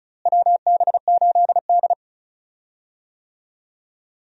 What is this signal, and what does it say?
Morse code, heard as a keyed tone, about 34 wpm, 700 Hz. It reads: WB8D
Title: USA Call Signs - 4 Characters - Repeated 3x - 35wpm